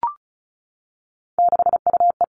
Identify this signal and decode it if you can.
Morse code, heard as a keyed tone, about 34 wpm, 700 Hz. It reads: 6UE
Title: All Cut Numbers - 3-Digit 35wpm